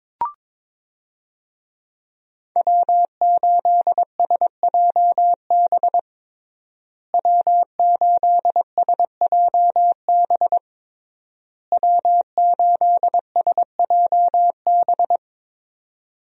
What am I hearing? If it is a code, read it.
Morse code, heard as a keyed tone, about 22 wpm, 700 Hz. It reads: W8SJB W8SJB W8SJB
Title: USA Call Signs - 5 Characters - Repeated 3x - 22wpm